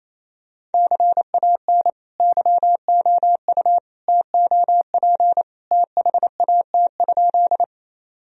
Morse code, heard as a keyed tone, about 28 wpm, 700 Hz. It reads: CAN YOU TOP THAT?
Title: Sentences from Top 1000 Words - Review Part 2 - 28wpm